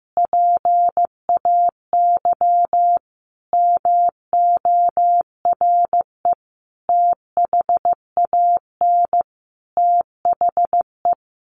Morse code, heard as a keyed tone, about 15 wpm, 700 Hz. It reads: PAY MORE THAN THE
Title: Sentences from Top 1100 Words 15wpm